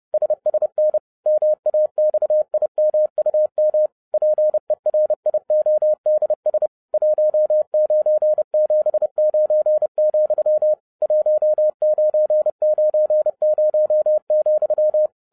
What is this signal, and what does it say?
Morse code, heard as a keyed tone, about 30 wpm, 610 Hz. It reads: SSN MAXIMUM PERIODS 1979, 1990,
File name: 2012-01-mp3 / n11-I DIDNT MISS THE SOLAR CYCLE 24-30wpm